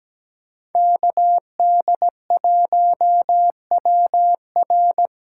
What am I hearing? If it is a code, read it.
Morse code, heard as a keyed tone, about 20 wpm, 700 Hz. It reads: KD1WR
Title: Speed Racing - USA 2x2 Call Signs 17wpm